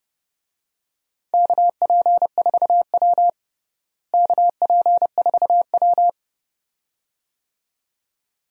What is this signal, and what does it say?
Morse code, heard as a keyed tone, about 30 wpm, 700 Hz. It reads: KP4W KP4W
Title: DXCC-100 Call Signs - 4 Characters - Repeated 2x - 30wpm